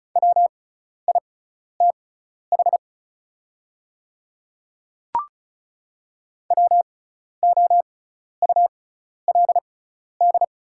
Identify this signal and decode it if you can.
Morse code, heard as a keyed tone, about 36 wpm, spaced as for 12 wpm, 690 Hz. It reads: WITH WOULD
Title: 6x Character Spacing - Top 300 Words - 35wpm